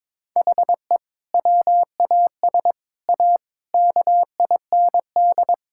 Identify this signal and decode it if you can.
Morse code, heard as a keyed tone, about 22 wpm, 700 Hz. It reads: HE WAS A KIND
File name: practice / 22wpm / Sentences with 3-Word Phrases - 22wpm